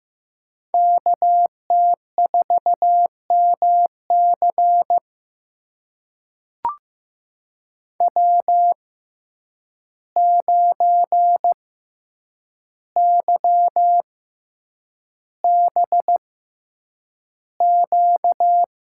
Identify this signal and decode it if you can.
Morse code, heard as a keyed tone, about 15 wpm, spaced as for 8 wpm, 700 Hz. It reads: KT4MC W9YBQ
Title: USA Call Signs - 5 Characters - 6x Character Spacing - 15wpm